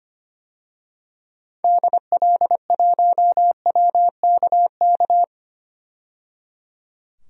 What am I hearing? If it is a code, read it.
Morse code, heard as a keyed tone, about 25 wpm, 700 Hz. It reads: DL1WKK
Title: DXCC-100 Call Signs - 6 Characters - 25wpm